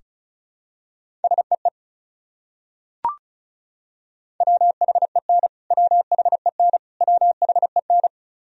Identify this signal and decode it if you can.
Morse code, heard as a keyed tone, about 34 wpm, 700 Hz. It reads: SEE WHEN WHEN WHEN
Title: Top 100 Words Repeated 35wpm